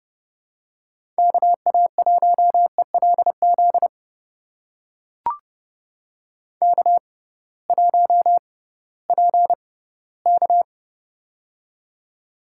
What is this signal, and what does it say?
Morse code, heard as a keyed tone, about 30 wpm, spaced as for 18 wpm, 700 Hz. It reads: KA1ELZ K1PK
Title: USA Call Signs - 6x Character Spacing - 30wpm